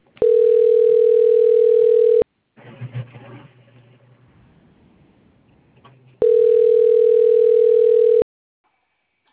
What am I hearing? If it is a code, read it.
Ringback tone (North America), 440 + 480 Hz.
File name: out- - - - - .48808